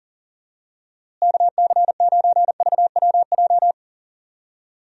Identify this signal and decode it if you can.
Morse code, heard as a keyed tone, about 40 wpm, 700 Hz. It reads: KC9VWJ
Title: USA Call Signs - 6 Characters - Repeated 3x - 40wpm